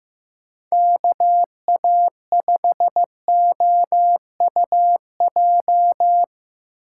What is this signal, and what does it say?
Morse code, heard as a keyed tone, about 15 wpm, 700 Hz. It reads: KA5OUJ
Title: USA Call Signs - 6 Characters - 2x Character Spacing - 15wpm